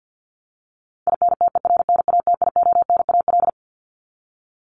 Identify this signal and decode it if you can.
Morse code, heard as a keyed tone, about 50 wpm, 700 Hz. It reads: INTERNATIONAL
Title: Top 501-750 Words 50wpm